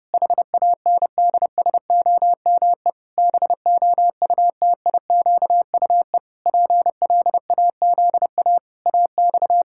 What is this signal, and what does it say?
Morse code, heard as a keyed tone, about 30 wpm, 700 Hz. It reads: HANDSOME BOUTIQUE PLAZA AX
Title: Pangram Challenge v2 - 30wpm